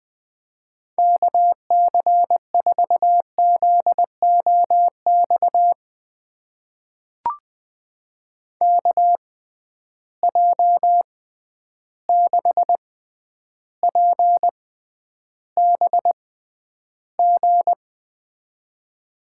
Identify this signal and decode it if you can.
Morse code, heard as a keyed tone, about 20 wpm, spaced as for 10 wpm, 700 Hz. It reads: KC4ZOX KJ6PBG
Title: USA Call Signs - 6 Characters - 6x Character Spacing - 20wpm